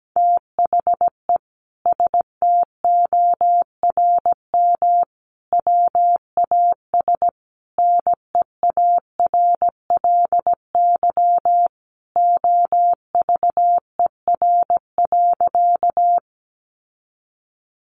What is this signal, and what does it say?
Morse code, heard as a keyed tone, about 17 wpm, 700 Hz. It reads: THE STORM WAS NEARLY OVER.
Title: Sentences from Top 1500 Words 17wpm